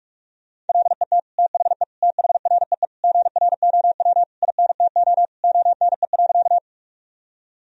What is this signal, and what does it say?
Morse code, heard as a keyed tone, about 46 wpm, 700 Hz. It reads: LET THE THREE GROW INTO ONE.
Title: Sentences from Top 1000 Words - Review Part 1 - 45wpm